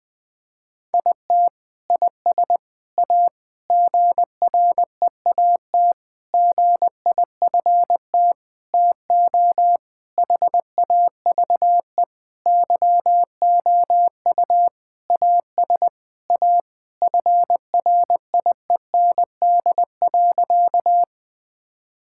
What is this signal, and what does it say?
Morse code, heard as a keyed tone, about 21 wpm, 700 Hz. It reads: IT IS A GREAT GIFT TO HAVE YOU AS A FRIEND.
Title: Sentences from Top 1000 Words - Review Part 6 - 20wpm